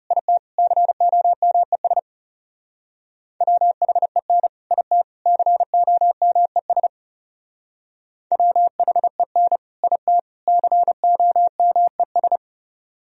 Morse code, heard as a keyed tone, about 37 wpm, 700 Hz. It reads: IT COMES WHEN IT COMES WHEN IT COMES
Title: Speed-Racing - Sets of 3 Words 30wpm